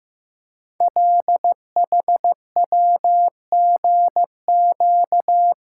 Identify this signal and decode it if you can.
Morse code, heard as a keyed tone, about 15 wpm, 700 Hz. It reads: LHWGQ
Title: Code Groups - 5 Letters - 15wpm